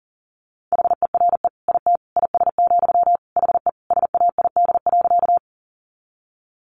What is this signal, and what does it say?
Morse code, heard as a keyed tone, about 40 wpm, 700 Hz. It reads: HERE IT IS, HE SAID.
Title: Sentences from Top 200 Words 40wpm